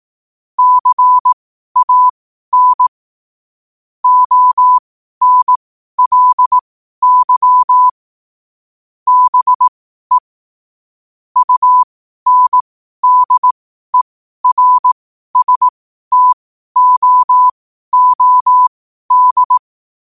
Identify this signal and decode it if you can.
Morse code, heard as a keyed tone, about 18 wpm, spaced as for 12 wpm, 1000 Hz. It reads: CAN ONLY BE UNDERSTOOD
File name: qotdfiles / 12wpm